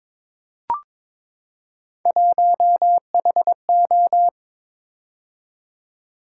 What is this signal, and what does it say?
Morse code, heard as a keyed tone, about 22 wpm, 700 Hz. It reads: 1HO
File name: practice / 22wpm / Three Letter-Number Combinations - 22wpm